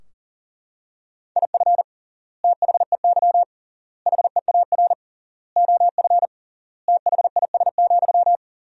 Morse code, heard as a keyed tone, about 40 wpm, 700 Hz. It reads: IF THEY HEAR OF THIS,
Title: 3x Word-Spacing - Sentences from Top 400 Words - 40wpm